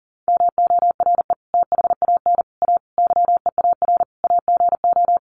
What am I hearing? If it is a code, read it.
Morse code, heard as a keyed tone, about 40 wpm, 700 Hz. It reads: MORE THAN A YEAR AGO
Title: Sets of 5 Words Encore2 40wpm